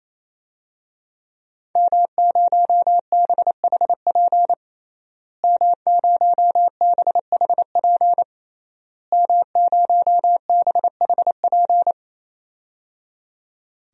Morse code, heard as a keyed tone, about 27 wpm, 700 Hz. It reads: M0BHP M0BHP M0BHP
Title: DXCC-100 Call Signs - 5 Characters - Repeated 3x - 28wpm